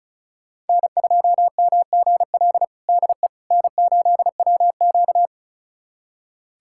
Morse code, heard as a keyed tone, about 36 wpm, 700 Hz. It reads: N2MGL DE N8WQ
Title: USA Callsign DE Callsign 35wpm